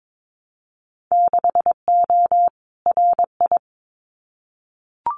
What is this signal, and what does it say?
Morse code, heard as a keyed tone, about 22 wpm, 700 Hz. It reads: 6O RI
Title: Winter Field Day Exchanges Encore 22wpm